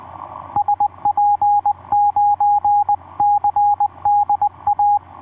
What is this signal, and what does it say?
Morse code, heard as a keyed tone, about 20 wpm, 820 Hz, with radio noise. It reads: SP9CDA
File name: v5-dk1ce sq7nsp etc 12m cw 02jan2011 1320